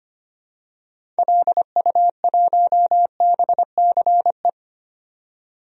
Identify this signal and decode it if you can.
Morse code, heard as a keyed tone, about 25 wpm, 700 Hz. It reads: LU1BCE
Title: DXCC-100 Call Signs - 6 Characters - 25wpm